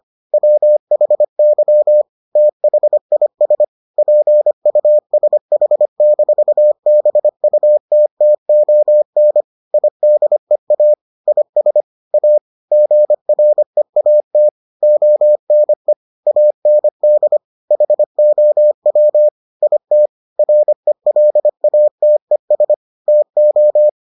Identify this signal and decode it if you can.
Morse code, heard as a keyed tone, about 25 wpm, 600 Hz. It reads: WHY THIS PUSH-BUTTON IDEA IS A GREAT ONE AND HOW IT RELATES TO